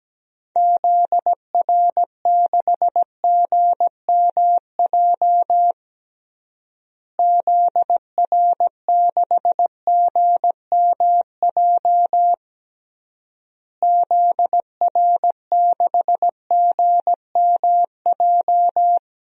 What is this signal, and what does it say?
Morse code, heard as a keyed tone, about 17 wpm, 700 Hz. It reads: ZR6GMJ ZR6GMJ ZR6GMJ
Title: DXCC-100 Call Signs - 6 Characters - Repeated 3x - 17wpm